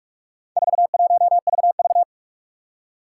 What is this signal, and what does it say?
Morse code, heard as a keyed tone, about 44 wpm, 700 Hz. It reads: 41VV
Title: Four Challenging Letter-Number Combinations 45wpm